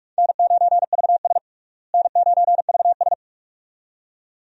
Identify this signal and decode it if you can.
Morse code, heard as a keyed tone, about 46 wpm, 700 Hz. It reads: N9VS N9VS
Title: USA Call Signs - 4 Characters - Repeated 2x - 45wpm